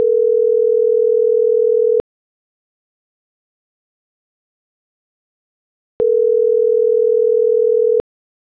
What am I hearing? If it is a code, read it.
Ringback tone (North America), 440 + 480 Hz.
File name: CENTRAL NMP / out marianacruznmp - .9845